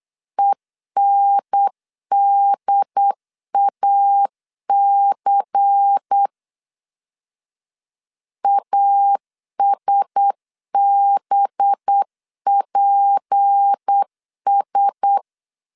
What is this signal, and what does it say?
Morse code, heard as a keyed tone, about 8 wpm, 800 Hz. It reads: ENDAC ASBPS
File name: 8wpm